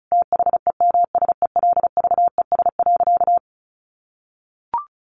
Morse code, heard as a keyed tone, about 34 wpm, 700 Hz. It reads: THEMSELVES.
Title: Sentences from Top 1100 Words 35wpm